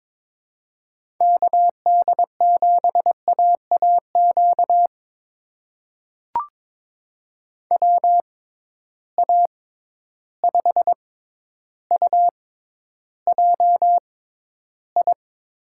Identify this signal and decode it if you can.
Morse code, heard as a keyed tone, about 22 wpm, spaced as for 11 wpm, 700 Hz. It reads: KD7AAQ WA5UJI
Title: USA Call Signs - 6 Characters - 6x Character Spacing - 22wpm